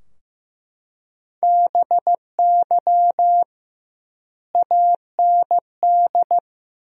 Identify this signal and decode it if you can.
Morse code, heard as a keyed tone, about 15 wpm, 700 Hz. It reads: BY AND
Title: 2x Word-Spacing - Sentences from Top 300 Words - 15wpm